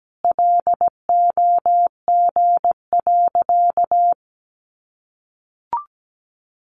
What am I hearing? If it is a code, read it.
Morse code, heard as a keyed tone, about 17 wpm, 700 Hz. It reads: LOG.
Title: Sentences from Top 1500 Words 17wpm